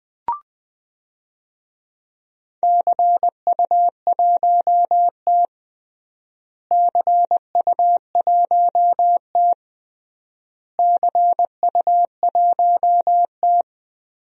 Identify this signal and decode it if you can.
Morse code, heard as a keyed tone, about 20 wpm, 700 Hz. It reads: CU1T CU1T CU1T
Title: DXCC-100 Call Signs - 4 Characters - Repeated 3x - 20wpm